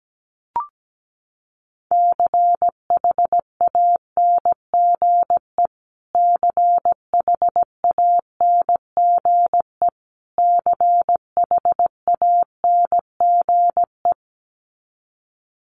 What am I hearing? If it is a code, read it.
Morse code, heard as a keyed tone, about 17 wpm, 700 Hz. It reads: CHANGE CHANGE CHANGE
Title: Top 500 Words Repeated 17wpm